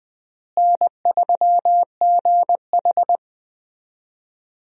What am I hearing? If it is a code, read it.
Morse code, heard as a keyed tone, about 20 wpm, 700 Hz. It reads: N3GH
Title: USA Call Signs - 4 Characters - Repeated 3x - 20wpm